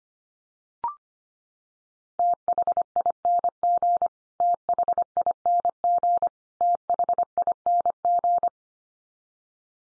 Morse code, heard as a keyed tone, about 25 wpm, 700 Hz. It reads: THING THING THING
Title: Top 100 Words 25wpm x3